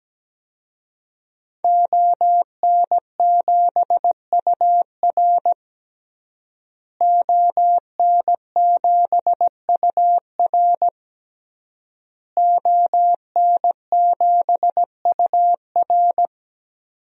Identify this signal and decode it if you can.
Morse code, heard as a keyed tone, about 17 wpm, 700 Hz. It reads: ON7UR ON7UR ON7UR
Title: DXCC-100 Call Signs - 5 Characters - Repeated 3x - 17wpm